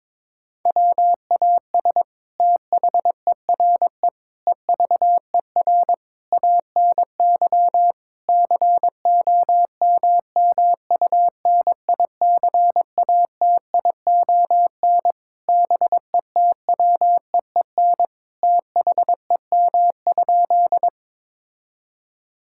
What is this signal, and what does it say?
Morse code, heard as a keyed tone, about 22 wpm, 700 Hz. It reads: WAS THERE EVER ANY COMMUNICATION BETWEEN THEM?
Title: Sentences from Top 1000 Words - Review Part 6 - 22wpm